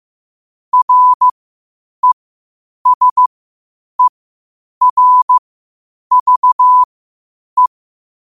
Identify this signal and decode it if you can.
Morse code, heard as a keyed tone, about 15 wpm, spaced as for 8 wpm, 1000 Hz. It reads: RESERVE
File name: qotdfiles / 8wpm